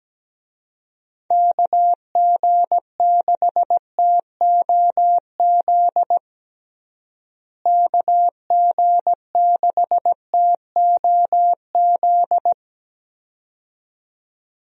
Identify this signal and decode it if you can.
Morse code, heard as a keyed tone, about 17 wpm, 700 Hz. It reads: KG6TOZ KG6TOZ
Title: USA Call Signs - 6 Characters - Repeated 2x - 17wpm